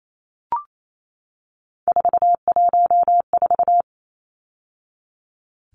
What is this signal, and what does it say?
Morse code, heard as a keyed tone, about 28 wpm, 700 Hz. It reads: 414
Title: Numbers - 3-Digit 28wpm